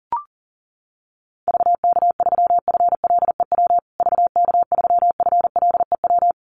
Morse code, heard as a keyed tone, about 40 wpm, 700 Hz. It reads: VK3FLEW VK3FLEW
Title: USA Call Signs Repeated 3x 40wpm